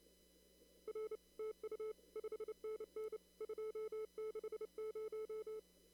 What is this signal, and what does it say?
Morse code, heard as a keyed tone, about 29 wpm, 440 Hz. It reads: R TU 5NN 260
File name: CW / kt5j wpx cw-op-k5pi